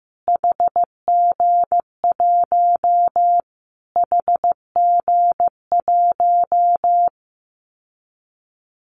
Morse code, heard as a keyed tone, about 15 wpm, 700 Hz. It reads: HG1 HG1
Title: Callsign Prefixes for Top 25 DXCC - Repeated 3x - 15wpm